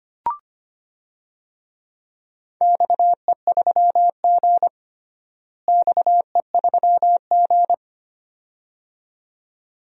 Morse code, heard as a keyed tone, about 25 wpm, 700 Hz. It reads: XE3G XE3G
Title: DXCC-100 Call Signs - 4 Characters - Repeated 2x - 25wpm